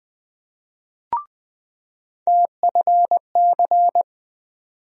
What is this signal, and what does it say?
Morse code, heard as a keyed tone, about 20 wpm, 700 Hz. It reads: TFC
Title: USA Call Sign Postfixes - 3 Letters 20wpm